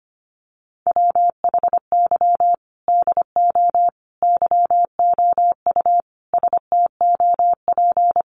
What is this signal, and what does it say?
Morse code, heard as a keyed tone, about 25 wpm, 700 Hz. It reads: WHY DO YOU STOP
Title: Sentences from Top 300 Words 25wpm